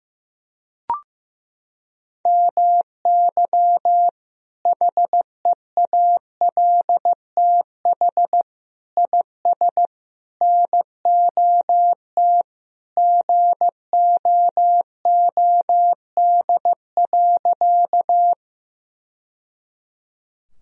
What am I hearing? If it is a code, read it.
Morse code, heard as a keyed tone, about 15 wpm, 700 Hz. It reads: MY HEALTH IS NOT GOOD.
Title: Sentences from Top 1000 Words - Review Part 3 - 15wpm